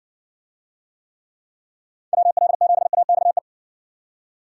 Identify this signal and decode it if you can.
Morse code, heard as a keyed tone, about 60 wpm, 700 Hz. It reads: WL7AXE